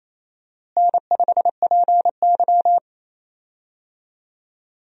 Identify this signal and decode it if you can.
Morse code, heard as a keyed tone, about 27 wpm, 700 Hz. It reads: N5PY
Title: USA Call Signs - 4 Characters - Repeated 3x - 28wpm